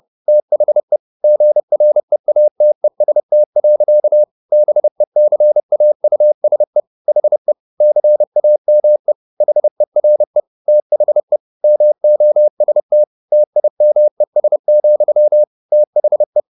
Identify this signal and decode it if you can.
Morse code, heard as a keyed tone, about 30 wpm, 600 Hz. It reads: THE GREATEST. BECAUSE HE CAME HERE THE MOST TIMES, THE
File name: the old man and the sea / the old man and the sea 1 30wpm